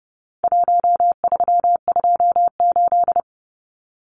Morse code, heard as a keyed tone, about 30 wpm, 700 Hz. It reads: 1328
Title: Numbers - 4-Digit 30wpm